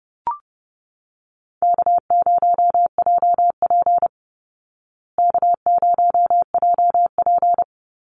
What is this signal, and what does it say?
Morse code, heard as a keyed tone, about 28 wpm, 700 Hz. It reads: K0JP K0JP
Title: Call Signs Repeated 2x - 30wpm